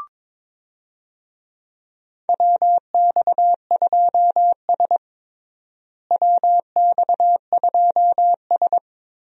Morse code, heard as a keyed tone, about 22 wpm, 700 Hz. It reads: WX2S WX2S
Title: USA Call Signs - 4 Characters - Repeated 3x - 22wpm